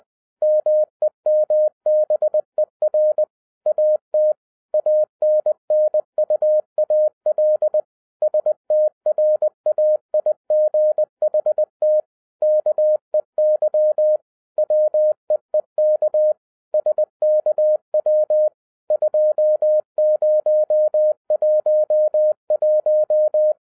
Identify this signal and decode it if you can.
Morse code, heard as a keyed tone, about 20 wpm, 610 Hz. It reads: MEMBER AT ANNUAL STRAIGHT KEY WEEK SKW 2011